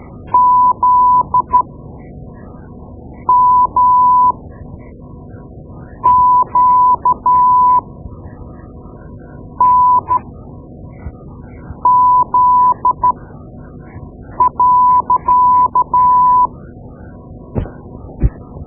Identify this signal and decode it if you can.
Morse code, heard as a keyed tone, about 13 wpm, spaced as for 4 wpm, 1000 Hz. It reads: ZMQNZ.
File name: Morse Lessons / Morse Lessons / Tape-2B